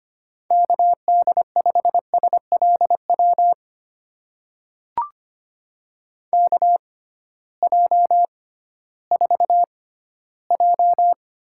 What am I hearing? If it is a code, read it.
Morse code, heard as a keyed tone, about 25 wpm, spaced as for 14 wpm, 700 Hz. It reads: KD5SLW KJ4J
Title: USA Call Signs - 6 Characters - 6x Character Spacing - 25wpm